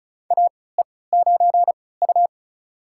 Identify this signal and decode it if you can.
Morse code, heard as a keyed tone, about 36 wpm, spaced as for 20 wpm, 700 Hz. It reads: AE9U
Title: USA Call Signs - 4 Characters - 3x Character Spacing - 35wpm